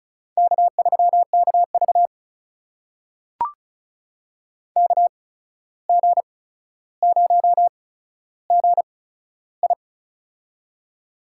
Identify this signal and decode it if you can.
Morse code, heard as a keyed tone, about 36 wpm, spaced as for 14 wpm, 700 Hz. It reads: K3KV KG0GI
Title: USA Call Signs - 8x Character Spacing - 35wpm